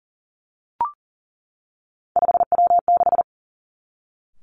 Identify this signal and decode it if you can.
Morse code, heard as a keyed tone, about 38 wpm, 700 Hz. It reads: 5W6
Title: Three Challenging Letter-Number Combinations 40wpm